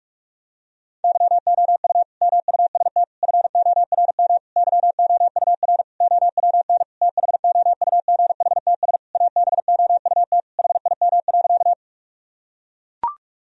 Morse code, heard as a keyed tone, about 41 wpm, 700 Hz. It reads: YOU MUST FORM YOUR OWN THOUGHTS ABOUT HIM.
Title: Sentences from Top 1000 Words - Review Part 1 - 45wpm